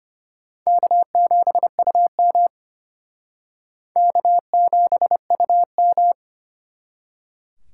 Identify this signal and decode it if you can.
Morse code, heard as a keyed tone, about 30 wpm, 700 Hz. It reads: K7UM K7UM
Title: Speed Racing - USA 1x2 Call Signs 25wpm